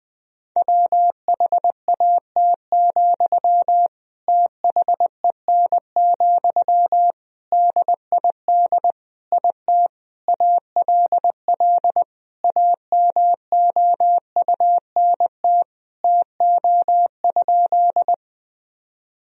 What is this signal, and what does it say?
Morse code, heard as a keyed tone, about 20 wpm, 700 Hz. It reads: WHAT, THEN, DID IT ALL AMOUNT TO?
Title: Sentences from Top 1000 Words - Review Part 4 - 20wpm